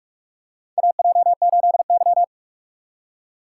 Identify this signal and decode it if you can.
Morse code, heard as a keyed tone, about 44 wpm, 700 Hz. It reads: AJ8Y
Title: USA Call Signs - 6x Character Spacing - 45wpm